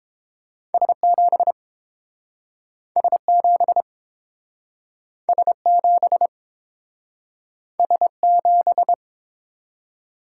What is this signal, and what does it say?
Morse code, heard as a keyed tone, about 30 wpm, 700 Hz. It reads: S7 S7 S7 S7